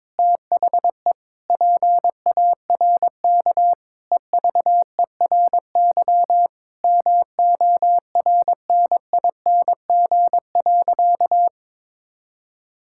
Morse code, heard as a keyed tone, about 22 wpm, 700 Hz. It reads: THE PARK EVERY MORNING.
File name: practice / 22wpm / Sentences with 2-Word Phrases - 22wpm